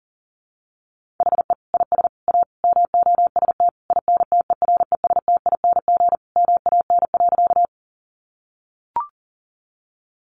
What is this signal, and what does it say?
Morse code, heard as a keyed tone, about 40 wpm, 700 Hz. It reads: HE IS A MOST INTERESTING MAN.
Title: Sentences from Top 1000 Words 40wpm